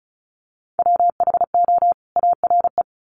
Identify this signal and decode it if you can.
Morse code, heard as a keyed tone, about 34 wpm, 700 Hz. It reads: WHO ARE
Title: Sets of 2 Words 35wpm